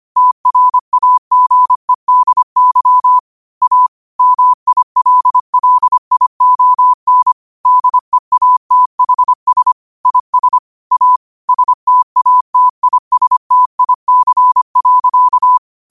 Morse code, heard as a keyed tone, about 25 wpm, 1000 Hz. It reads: TRAGEDY A MILLION DEATHS IS A STATISTIC.
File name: qotdfiles / 25wpm